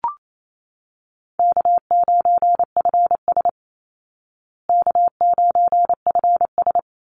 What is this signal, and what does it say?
Morse code, heard as a keyed tone, about 28 wpm, 700 Hz. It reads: K9FS K9FS
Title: Easy Call Signs Repeated 2x - 28wpm